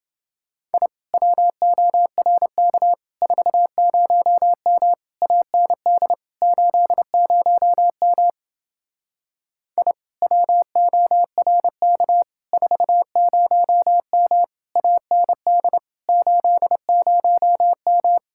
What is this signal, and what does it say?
Morse code, heard as a keyed tone, about 28 wpm, 700 Hz. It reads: I WORK 40M AND 80M I WORK 40M AND 80M